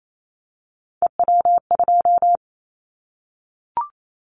Morse code, heard as a keyed tone, about 28 wpm, 700 Hz. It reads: EW2
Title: Callsign Prefixes for Top 50 DXCC - 28wpm